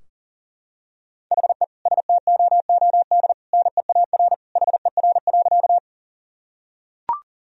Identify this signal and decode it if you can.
Morse code, heard as a keyed tone, about 37 wpm, 700 Hz. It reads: HE STOOD NEAR HER.
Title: Sentences from Top 1000 Words - Review Part 2 - 40wpm